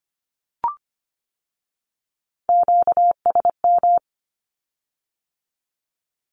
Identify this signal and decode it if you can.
Morse code, heard as a keyed tone, about 25 wpm, 700 Hz. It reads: QSM
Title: Q-Codes with Spoken Meaning - 25wpm